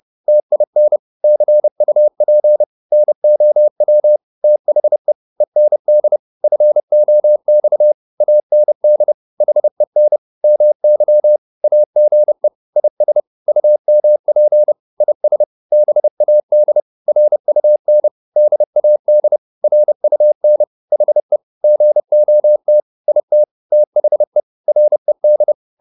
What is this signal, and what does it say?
Morse code, heard as a keyed tone, about 30 wpm, 600 Hz. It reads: TIN CUP NOW THE END FOX AND HEN MY AGE IS UMP IS BAD RUN DAD RUN HE GOT IT THE RED